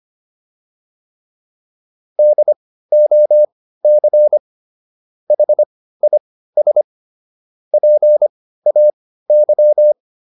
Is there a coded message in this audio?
Yes: Morse code from 2.2 to 9.9 s, heard as a keyed tone, about 25 wpm, spaced as for 15 wpm, 600 Hz. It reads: DOC HIS PAY